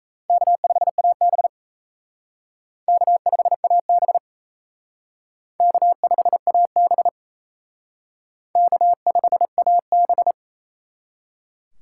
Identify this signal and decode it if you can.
Morse code, heard as a keyed tone, about 34 wpm, 700 Hz. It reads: K5AB K5AB K5AB K5AB
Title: Speed Racing - USA 1x2 Call Signs 28wpm